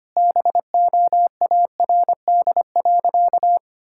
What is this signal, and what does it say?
Morse code, heard as a keyed tone, about 25 wpm, 700 Hz. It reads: BOARD.
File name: practice / 25wpm / Sentences with 3-Word Phrases - 25wpm